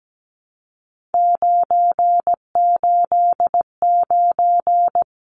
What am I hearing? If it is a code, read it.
Morse code, heard as a keyed tone, about 17 wpm, 700 Hz. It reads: 989
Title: Numbers - 3-Digit 17wpm